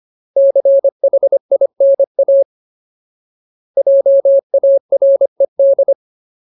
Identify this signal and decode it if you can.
Morse code, heard as a keyed tone, about 25 wpm, 550 Hz. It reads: CHINA JARED
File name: Ultimate Word List / 25WPM Ultimate